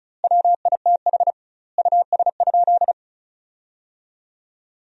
Morse code, heard as a keyed tone, about 34 wpm, 700 Hz. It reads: WITH US?
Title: 2x Word-Spacing - Sentences from Top 200 Words - 35wpm